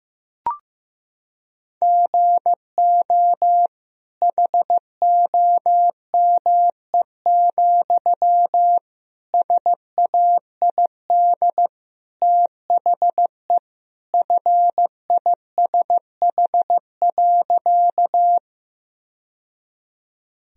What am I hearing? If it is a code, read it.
Morse code, heard as a keyed tone, about 15 wpm, 700 Hz. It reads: GO HOME, SAID THE FISH.
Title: Sentences from Top 1000 Words - Review Part 1 - 15wpm